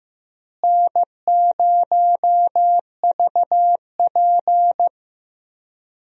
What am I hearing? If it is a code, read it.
Morse code, heard as a keyed tone, about 15 wpm, 700 Hz. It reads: N0VP
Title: USA Call Signs - 4 Characters - Repeated 2x - 15wpm